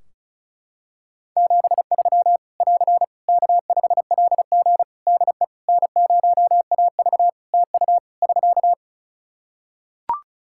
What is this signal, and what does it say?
Morse code, heard as a keyed tone, about 36 wpm, 700 Hz. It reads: 73 <AR> K5LG DE N0AV TU <SK>
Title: K1USN SST QSO Practice 35wpm